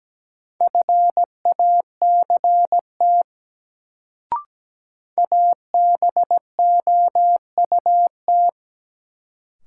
Morse code, heard as a keyed tone, about 17 wpm, 700 Hz. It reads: FACT ABOUT
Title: Sets of 2 Words - New Word Drill - 17wpm